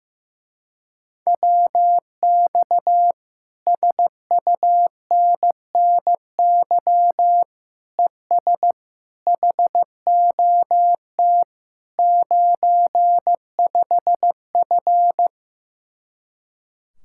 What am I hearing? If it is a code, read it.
Morse code, heard as a keyed tone, about 15 wpm, 700 Hz. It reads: WX SUNNY ES HOT 95F
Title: Speed-Racing Common QSO Phrases Encore - 15wpm